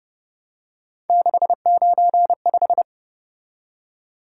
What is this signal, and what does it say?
Morse code, heard as a keyed tone, about 31 wpm, 700 Hz. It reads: 695
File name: Speed-Racing - Three Letter-Number Combinations - 25wpm